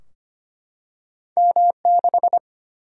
Morse code, heard as a keyed tone, about 25 wpm, 700 Hz. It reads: M6
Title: Single-Letter Single-Number 25wpm